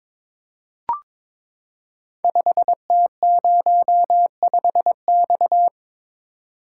Morse code, heard as a keyed tone, about 22 wpm, 700 Hz. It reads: 5T05X
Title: Code Groups - 5 Letter:Numbers - 22wpm